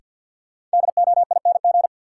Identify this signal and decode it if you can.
Morse code, heard as a keyed tone, about 47 wpm, 700 Hz. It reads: DOING
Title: Sentences from Top 1400 Words - New Word Drill - 50wpm